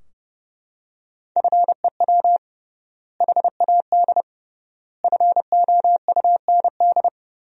Morse code, heard as a keyed tone, about 30 wpm, 700 Hz. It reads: FEW HAD FOUND
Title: 3x Word-Spacing - Sentences from Top 300 Words - 30wpm